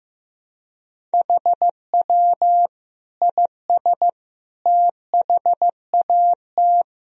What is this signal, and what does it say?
Morse code, heard as a keyed tone, about 15 wpm, 700 Hz. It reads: HW IS THAT
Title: Sentences from Top 500 Words and Abbreviations 15wpm